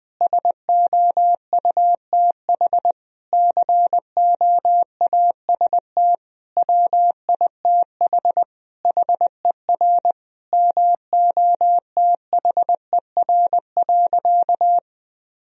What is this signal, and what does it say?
Morse code, heard as a keyed tone, about 20 wpm, 700 Hz. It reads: SOUTH COAST WITH HER MOTHER.
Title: Sentences from Top 1000 Words - Review Part 4 - 20wpm